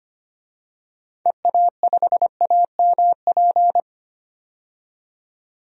Morse code, heard as a keyed tone, about 25 wpm, 700 Hz. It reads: EA5AMP
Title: DXCC-100 Call Signs - 6 Characters - 25wpm